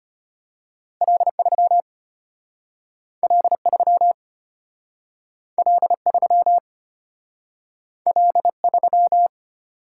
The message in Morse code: L3 L3 L3 L3

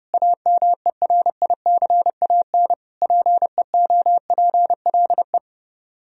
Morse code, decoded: AMERICAN PEOPLE

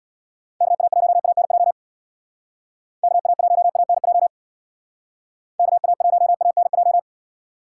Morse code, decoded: KA1ANJ KA1ANJ KA1ANJ

